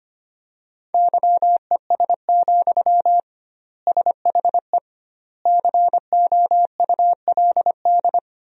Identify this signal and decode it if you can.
Morse code, heard as a keyed tone, about 25 wpm, 700 Hz. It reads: YES, SHE COULD
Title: 2x Word-Spacing - Sentences from Top 500 Words - 25wpm